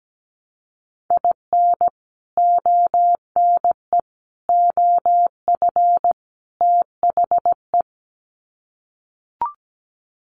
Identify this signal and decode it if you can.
Morse code, heard as a keyed tone, about 17 wpm, 700 Hz. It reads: IN ONE OF THE
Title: Sets of 4 Words 17wpm